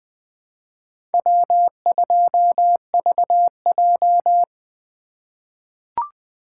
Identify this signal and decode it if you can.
Morse code, heard as a keyed tone, about 20 wpm, 700 Hz. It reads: W2VJ
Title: USA Call Signs - 4 Characters - 3x Character Spacing - 20wpm